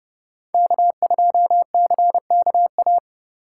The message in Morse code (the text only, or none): K2CKA